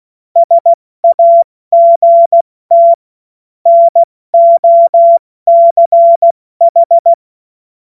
Morse code, heard as a keyed tone, about 16 wpm, 680 Hz. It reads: SAGT NOCH
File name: Morsekurs / Witze / Witz 11. Mai 2021, 70 F80